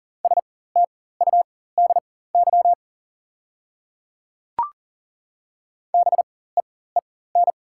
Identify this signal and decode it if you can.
Morse code, heard as a keyed tone, about 39 wpm, spaced as for 19 wpm, 700 Hz. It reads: STUDY BEEN
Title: 4x Character Spacing - Top 300 Words - 40wpm